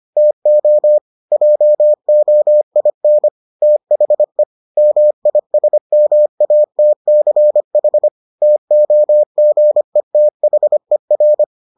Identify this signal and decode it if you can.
Morse code, heard as a keyed tone, about 25 wpm, 600 Hz. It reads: TO JOIN THE MISMATCH TOGETHER